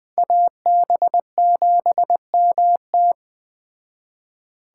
Morse code, heard as a keyed tone, about 20 wpm, 700 Hz. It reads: AB7MT
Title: Speed Racing - USA Call Signs 17wpm